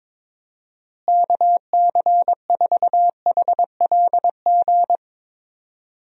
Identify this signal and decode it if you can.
Morse code, heard as a keyed tone, about 22 wpm, 700 Hz. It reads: KC4HLG
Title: USA Call Signs - 6 Characters - Repeated 3x - 22wpm